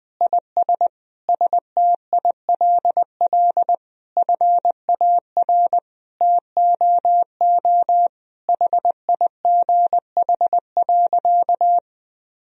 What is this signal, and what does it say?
Morse code, heard as a keyed tone, about 20 wpm, 700 Hz. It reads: IS STILL FAR TOO HIGH.
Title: Sentences from Top 1000 Words - Review Part 1 - 20wpm